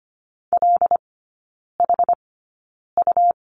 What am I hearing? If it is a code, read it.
Morse code, heard as a keyed tone, about 25 wpm, spaced as for 9 wpm, 700 Hz. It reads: LHU
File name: Lesson 17e - Letter u - Three Characters 25wpm